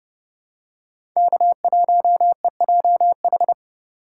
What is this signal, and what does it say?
Morse code, heard as a keyed tone, about 30 wpm, 700 Hz. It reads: K1EJH
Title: USA Call Signs - 2x Character Spacing - 30wpm